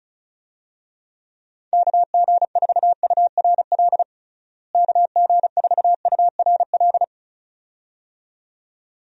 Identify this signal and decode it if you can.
Morse code, heard as a keyed tone, about 36 wpm, 700 Hz. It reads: KG4URL KG4URL
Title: USA Call Signs - 6 Characters - Repeated 2x - 35wpm